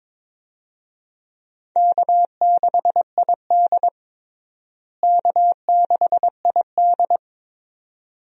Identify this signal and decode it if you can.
Morse code, heard as a keyed tone, about 22 wpm, 700 Hz. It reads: K6ID K6ID
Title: USA Call Signs - 4 Characters - Repeated 2x - 22wpm